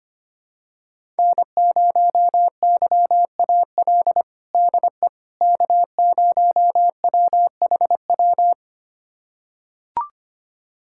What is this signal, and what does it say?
Morse code, heard as a keyed tone, about 25 wpm, 700 Hz. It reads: N0YAL DE K0WHW
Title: USA Callsign DE Callsign - 5 Characters - 25wpm